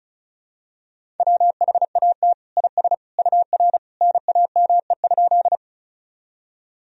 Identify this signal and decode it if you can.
Morse code, heard as a keyed tone, about 36 wpm, 700 Hz. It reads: WHAT IS UR NAME?
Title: Sentences from Top 200 Words and Abbreviations 35wpm